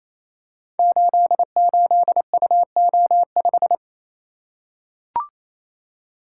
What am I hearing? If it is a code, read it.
Morse code, heard as a keyed tone, about 28 wpm, 700 Hz. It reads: 88UO5
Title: Code Groups - 5 Letter:Numbers - 28wpm